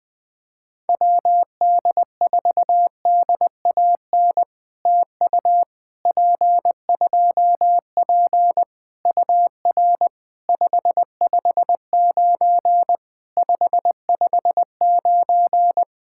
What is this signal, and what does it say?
Morse code, heard as a keyed tone, about 20 wpm, 700 Hz. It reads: WD4DAN TU P2P UR 559 559
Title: POTA QSO Practice - 20wpm